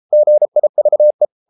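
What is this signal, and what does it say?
Morse code, heard as a keyed tone, about 33 wpm, 600 Hz. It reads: GIVE
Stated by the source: Morse code at 33 wpm, QRQ.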